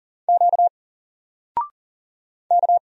Morse code, heard as a keyed tone, about 40 wpm, 700 Hz. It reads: Q K
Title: Single Letter-Number Alt 40wpm